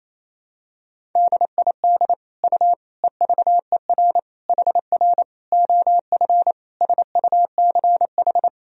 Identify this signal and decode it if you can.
Morse code, heard as a keyed tone, about 27 wpm, 700 Hz. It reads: DID U EVER HR OF SUCH
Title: Sentences from Top 400 Words and Abbreviations 28wpm